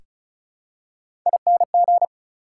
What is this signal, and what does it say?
Morse code, heard as a keyed tone, about 34 wpm, 700 Hz. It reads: ING
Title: Prefixes and Suffixes 35wpm